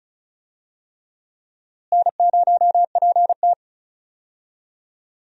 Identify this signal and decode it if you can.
Morse code, heard as a keyed tone, about 36 wpm, 700 Hz. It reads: N0PT
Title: USA Call Signs - 4 Characters - Repeated 3x - 35wpm